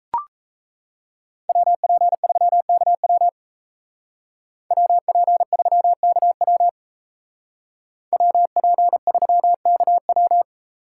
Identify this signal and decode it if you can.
Morse code, heard as a keyed tone, about 33 wpm, 700 Hz. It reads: WP3KW WP3KW WP3KW
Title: Speed Racing - USA 2x2 Call Signs 28wpm